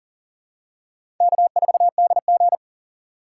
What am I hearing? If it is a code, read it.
Morse code, heard as a keyed tone, about 37 wpm, 700 Hz. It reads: K4DG
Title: USA Call Signs - 3x Character Spacing - 40wpm